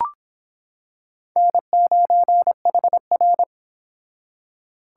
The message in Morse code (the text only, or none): N9HR